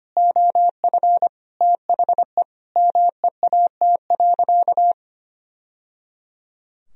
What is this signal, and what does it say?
Morse code, heard as a keyed tone, about 25 wpm, 700 Hz. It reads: OF THE MEAT.
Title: Sentences from Top 1000 Words - Review Part 3 - 25wpm